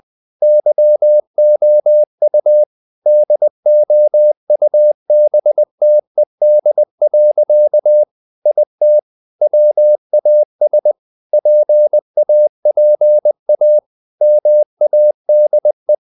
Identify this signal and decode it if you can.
Morse code, heard as a keyed tone, about 20 wpm, 600 Hz. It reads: YOU DOUBTED. IT WAS PAPA MADE
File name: the old man and the sea / the old man and the sea part 1 20wpm